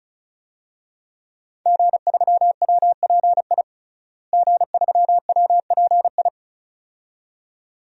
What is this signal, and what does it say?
Morse code, heard as a keyed tone, about 34 wpm, 700 Hz. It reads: G3WPI G3WPI
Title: DXCC-100 Call Signs - 5 Characters - Repeated 2x - 35wpm